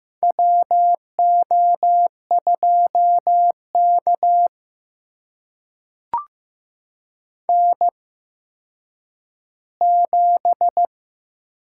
Morse code, heard as a keyed tone, about 15 wpm, spaced as for 9 wpm, 700 Hz. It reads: WO2K N7
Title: USA Call Signs - 8x Character Spacing - 15wpm